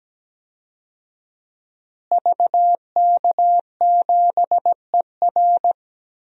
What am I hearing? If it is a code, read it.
Morse code, heard as a keyed tone, about 17 wpm, 700 Hz. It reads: VK7ER
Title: DXCC-100 Call Signs - 5 Characters - Repeated 2x - 17wpm